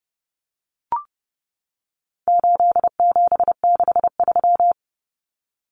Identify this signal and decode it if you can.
Morse code, heard as a keyed tone, about 30 wpm, 700 Hz. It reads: 8763
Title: Common Cut Numbers - 4-Digit 30wpm